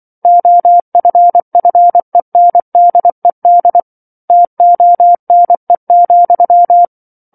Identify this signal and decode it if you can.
Morse code, heard as a keyed tone, about 24 wpm, 700 Hz. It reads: OFFENDED TONE,
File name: 07-Chapter-07 24 WPM